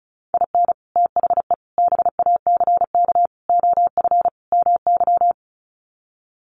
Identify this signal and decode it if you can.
Morse code, heard as a keyed tone, about 34 wpm, 700 Hz. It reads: IN THE BACK OF MY
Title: Sets of 5 Words Encore 35wpm